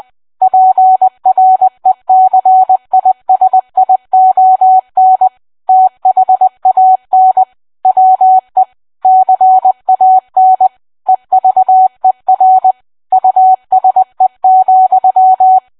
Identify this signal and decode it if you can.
Morse code, heard as a keyed tone, about 20 wpm, 750 Hz. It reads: PRECISION THAN WE CAN EVER USE,